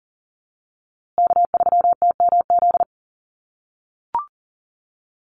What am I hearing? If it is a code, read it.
Morse code, heard as a keyed tone, about 40 wpm, 700 Hz. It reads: K3TMZ
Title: USA Call Signs Repeated 2x 40wpm